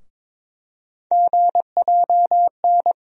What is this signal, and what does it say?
Morse code, heard as a keyed tone, about 22 wpm, 700 Hz. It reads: GJN
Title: USA Call Sign Postfixes - 3 Letters 22wpm